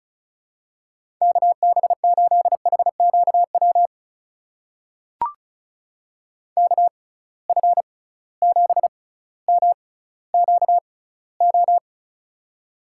Morse code, heard as a keyed tone, about 36 wpm, spaced as for 18 wpm, 700 Hz. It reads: KB8HQW KF7MQO
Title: USA Call Signs - 6 Characters - 6x Character Spacing - 35wpm